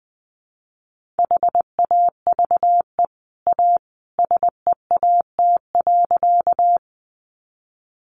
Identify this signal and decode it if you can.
Morse code, heard as a keyed tone, about 20 wpm, 700 Hz. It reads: HAVE A SEAT.
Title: Sentences from Top 1500 Words 20wpm